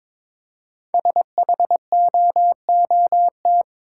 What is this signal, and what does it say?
Morse code, heard as a keyed tone, about 22 wpm, 700 Hz. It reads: SHOOT
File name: practice / 22wpm / 5 Letter Words - First 3 Letters Given - 22wpm